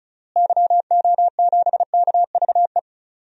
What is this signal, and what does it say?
Morse code, heard as a keyed tone, about 35 wpm, 700 Hz. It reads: YO7KVE
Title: DXCC-100 Call Signs - 6 Characters - Repeated 3x - 35wpm